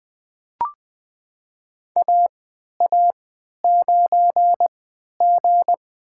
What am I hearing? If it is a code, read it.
Morse code, heard as a keyed tone, about 20 wpm, spaced as for 11 wpm, 700 Hz. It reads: AA9G
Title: USA Call Signs - 4 Characters - 3x Character Spacing - 20wpm